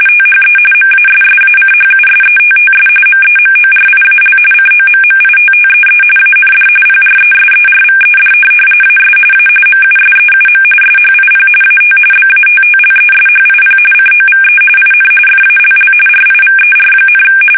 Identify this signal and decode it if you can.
RTTY, about 45.45 baud. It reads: THE MASSACHUSETTS MEMORIAL COMMITTEE AND PRESERVED AS A MUSEUM SHIP AT BATTLESHIP COVE IN FALL RIVER,
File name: fsk850 / USS-Massachusetts